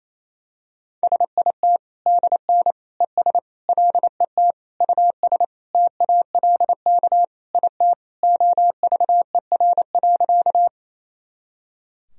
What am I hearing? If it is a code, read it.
Morse code, heard as a keyed tone, about 28 wpm, 700 Hz. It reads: SIT DN ES LET US TALK IT OVER.
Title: Sentences from Top 500 Words and Abbreviations 28wpm